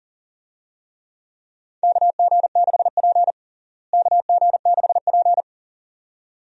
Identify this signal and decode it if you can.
Morse code, heard as a keyed tone, about 40 wpm, 700 Hz. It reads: KG6P KG6P
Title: USA Call Signs - 4 Characters - Repeated 2x - 40wpm